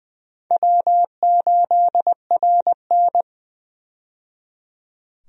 Morse code, heard as a keyed tone, about 21 wpm, 700 Hz. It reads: W8RN
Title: USA Call Signs - 4 Characters - 20wpm